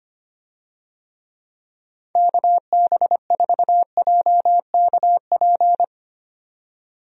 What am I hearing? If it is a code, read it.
Morse code, heard as a keyed tone, about 25 wpm, 700 Hz. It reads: KB4JKP
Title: USA Call Signs - 6 Characters - 25wpm